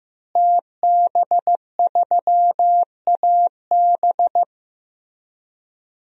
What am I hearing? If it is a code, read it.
Morse code, heard as a keyed tone, about 15 wpm, 700 Hz. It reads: TB3AB
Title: DXCC-100 Call Signs - 5 Characters - Repeated 3x - 15wpm